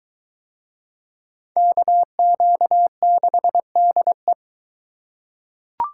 Morse code, heard as a keyed tone, about 22 wpm, 700 Hz. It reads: KQ6DE